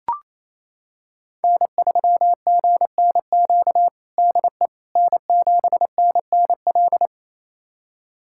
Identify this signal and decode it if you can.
Morse code, heard as a keyed tone, about 28 wpm, 700 Hz. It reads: N3GNQ DE N7NNL
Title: USA Callsign DE Callsign - 5 Characters - 28wpm